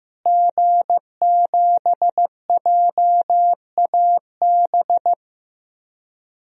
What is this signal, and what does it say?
Morse code, heard as a keyed tone, about 15 wpm, 700 Hz. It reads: G7JAB
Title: DXCC-100 Call Signs - 5 Characters - Repeated 2x - 15wpm